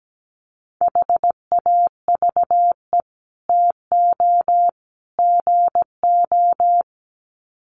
Morse code, heard as a keyed tone, about 17 wpm, 700 Hz. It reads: HAVE TO GO
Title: Sets of 3 Words 17wpm